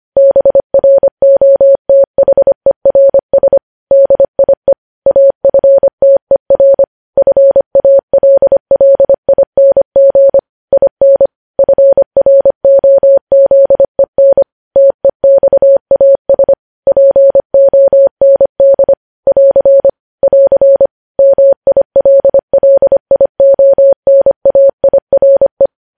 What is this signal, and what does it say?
Morse code, heard as a keyed tone, about 25 wpm, 550 Hz. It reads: BROTHERS DIE AFTER FALLING IN FROZEN TEXAS POND <AR> <AR> MILLIONAIRE
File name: podcast / News headlines-1x-25x25